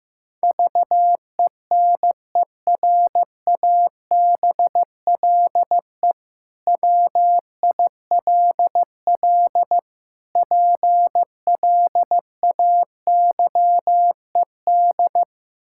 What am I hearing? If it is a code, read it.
Morse code, heard as a keyed tone, about 15 wpm, 700 Hz. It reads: VENERABLE WILL PLAYED
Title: Pangram Challenge v2 - 15wpm